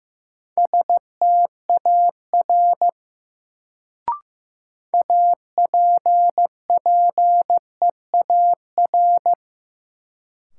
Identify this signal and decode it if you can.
Morse code, heard as a keyed tone, about 15 wpm, 700 Hz. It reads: STAR APPEAR
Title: Sentences from Top 1000 Words - Review Part 2 - New Word Drill - 15wpm